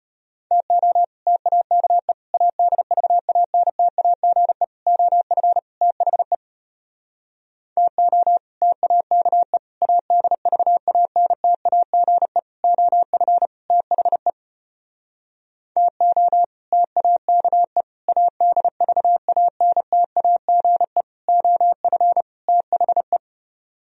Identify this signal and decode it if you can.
Morse code, heard as a keyed tone, about 33 wpm, 700 Hz. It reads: TO TAKE ADVANTAGE OF THE TO TAKE ADVANTAGE OF THE TO TAKE ADVANTAGE OF THE